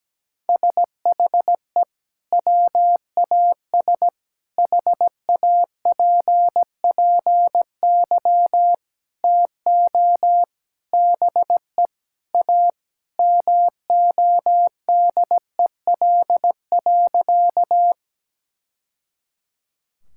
Morse code, heard as a keyed tone, about 17 wpm, 700 Hz. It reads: SHE WAS HAPPY TO BE A MODEL.
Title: Sentences from Top 1000 Words - Review Part 5 - 17wpm